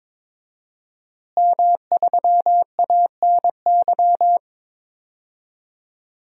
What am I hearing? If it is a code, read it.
Morse code, heard as a keyed tone, about 22 wpm, 700 Hz. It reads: M3ANY
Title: DXCC-100 Call Signs - 5 Characters - Repeated 3x - 22wpm